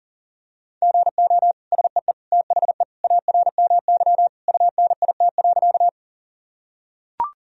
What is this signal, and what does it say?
Morse code, heard as a keyed tone, about 37 wpm, 700 Hz. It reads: GO SEE THE ARMY UNIT.
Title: Sentences from Top 1000 Words - Review Part 2 - 40wpm